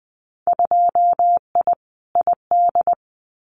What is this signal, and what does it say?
Morse code, heard as a keyed tone, about 20 wpm, 700 Hz. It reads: 2I ID
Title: Winter Field Day Exchanges Encore 20wpm